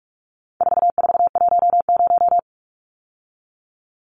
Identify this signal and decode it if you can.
Morse code, heard as a keyed tone, about 44 wpm, 700 Hz. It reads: 4410
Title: Numbers - 4-Digit 45wpm